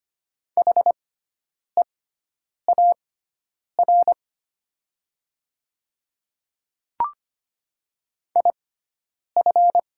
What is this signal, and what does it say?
Morse code, heard as a keyed tone, about 25 wpm, spaced as for 9 wpm, 700 Hz. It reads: HEAR IF
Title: 6x Character Spacing - Top 300 Words - 25wpm